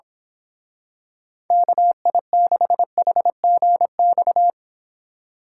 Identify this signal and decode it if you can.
Morse code, heard as a keyed tone, about 26 wpm, 700 Hz. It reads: KI6HGX
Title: Speed Racing - USA Call Signs 22wpm